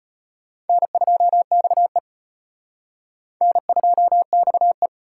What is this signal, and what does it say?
Morse code, heard as a keyed tone, about 30 wpm, 700 Hz. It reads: N2XE N2XE
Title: Speed Racing - USA 1x2 Call Signs 25wpm